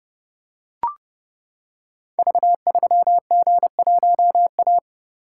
Morse code, heard as a keyed tone, about 30 wpm, 700 Hz. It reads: V3G1A
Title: Code Groups - 5 Letter:Numbers - 30wpm